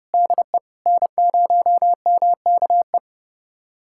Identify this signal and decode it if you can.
Morse code, heard as a keyed tone, about 30 wpm, 700 Hz. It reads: DE N0MKE
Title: USA Callsign DE Callsign - 5 Characters - 30wpm